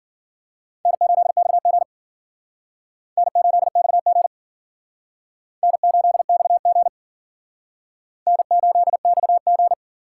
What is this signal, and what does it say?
Morse code, heard as a keyed tone, about 51 wpm, 700 Hz. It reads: N8XG N8XG N8XG N8XG